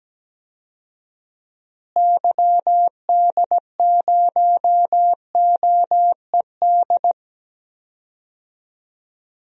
Morse code, heard as a keyed tone, about 17 wpm, 700 Hz. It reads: YD0OED
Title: DXCC-100 Call Signs - 6 Characters - Repeated 3x - 17wpm